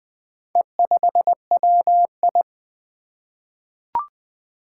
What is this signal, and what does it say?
Morse code, heard as a keyed tone, about 20 wpm, 700 Hz. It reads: E5WI
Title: Four Challenging Letter-Number Combinations 20wpm